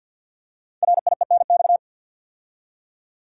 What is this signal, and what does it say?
Morse code, heard as a keyed tone, about 52 wpm, 700 Hz. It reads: RIENX
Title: Code Groups - 5 Letters - 50wpm